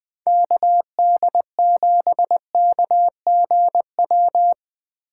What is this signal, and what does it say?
Morse code, heard as a keyed tone, about 20 wpm, 700 Hz. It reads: KD7KGW
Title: USA Call Signs - 6 Characters - 6x Character Spacing - 20wpm